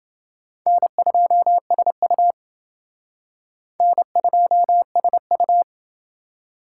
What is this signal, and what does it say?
Morse code, heard as a keyed tone, about 29 wpm, 700 Hz. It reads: N2SU N2SU